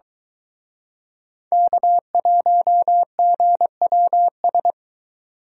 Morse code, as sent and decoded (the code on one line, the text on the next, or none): -.- .---- --. .-- ...
K1GWS